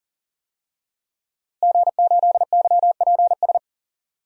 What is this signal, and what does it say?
Morse code, heard as a keyed tone, about 40 wpm, 700 Hz. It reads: G8YPS
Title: DXCC-100 Call Signs - 5 Characters - Repeated 2x - 40wpm